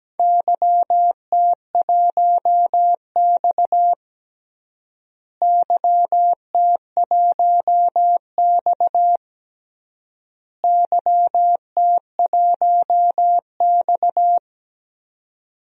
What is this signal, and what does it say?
Morse code, heard as a keyed tone, about 17 wpm, 700 Hz. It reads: YT1X YT1X YT1X
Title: DXCC-100 Call Signs - 4 Characters - Repeated 3x - 17wpm